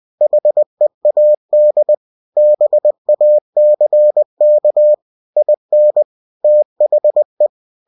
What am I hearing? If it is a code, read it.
Morse code, heard as a keyed tone, about 20 wpm, 600 Hz. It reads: HEAD BACK IN THE
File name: cw-books / winnie-the-pooh / winnie-20wpm-page-0014